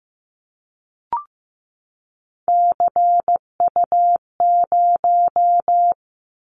Morse code, heard as a keyed tone, about 15 wpm, 700 Hz. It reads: CU0
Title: Callsign Prefixes for Top 50 DXCC - 15wpm